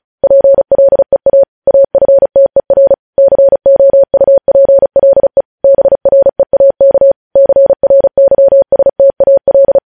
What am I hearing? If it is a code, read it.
Morse code, heard as a keyed tone, about 36 wpm, 550 Hz. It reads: PLEA AFTER COUPLE BREAK CRYSTAL
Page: news headlines in morse code at 35 wpm